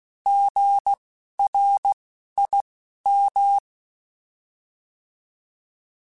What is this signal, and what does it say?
Morse code, heard as a keyed tone, about 16 wpm, spaced as for 11 wpm, 800 Hz. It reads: GRIM
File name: Alphabetical / Alphabetical mp3s / Gs @10